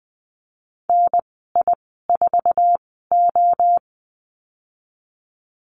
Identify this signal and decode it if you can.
Morse code, heard as a keyed tone, about 20 wpm, spaced as for 15 wpm, 700 Hz. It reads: NI4O
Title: 088 Lesson 15g - Number 5 - Call Signs 20wpm